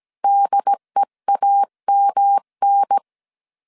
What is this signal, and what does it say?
Morse code, heard as a keyed tone, about 17 wpm, 800 Hz. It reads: DEAMN
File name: CW / DET / 1 3 1500C S A 16wpm 2141